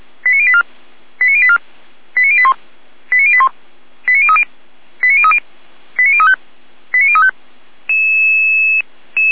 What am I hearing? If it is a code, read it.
Five-tone selective call (ZVEI), heard as quick runs of five tones, tones of about 70 ms. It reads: 89084 89084 89081 89081 89030 89030 89035 89035